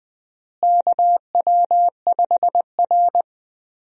Morse code, heard as a keyed tone, about 20 wpm, 700 Hz. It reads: KW5R
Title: USA Call Signs - 4 Characters - Repeated 3x - 20wpm